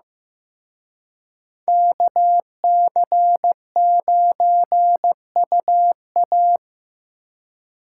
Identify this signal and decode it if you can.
Morse code, heard as a keyed tone, about 18 wpm, 700 Hz. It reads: KC9UA
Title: Speed Racing - USA 2x2 Call Signs 15wpm